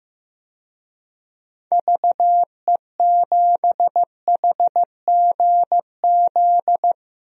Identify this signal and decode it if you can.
Morse code, heard as a keyed tone, about 16 wpm, 700 Hz. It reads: VE7HGZ
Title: DXCC-100 Call Signs - 6 Characters - Repeated 2x - 15wpm